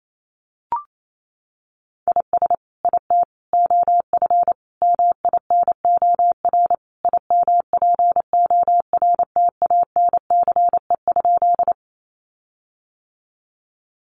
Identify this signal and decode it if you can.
Morse code, heard as a keyed tone, about 28 wpm, 700 Hz. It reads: IS IT OF MINOR IMPORTANCE?
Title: Sentences from Top 1500 Words 28wpm